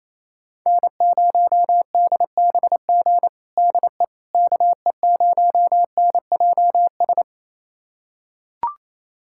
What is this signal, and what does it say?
Morse code, heard as a keyed tone, about 28 wpm, 700 Hz. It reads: N0DBG DE KE0NJS
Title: USA Callsign DE Callsign 28wpm